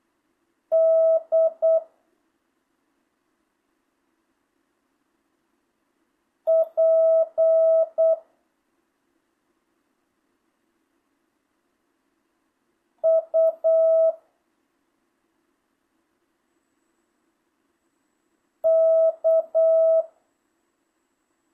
Morse code, heard as a keyed tone, about 8 wpm, 650 Hz. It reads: D P U K
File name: jh8cbh / 2025cw